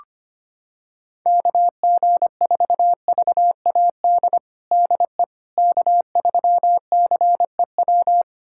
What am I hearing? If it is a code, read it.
Morse code, heard as a keyed tone, about 25 wpm, 700 Hz. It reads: KG4VAD DE K3CEW
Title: USA Callsign DE Callsign 25wpm